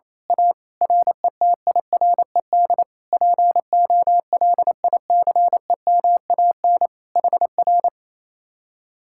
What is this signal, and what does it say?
Morse code, heard as a keyed tone, about 26 wpm, 700 Hz. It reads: A RETIRED POLICEMAN HR
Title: Speed-Racing Common QSO Phrases - 28wpm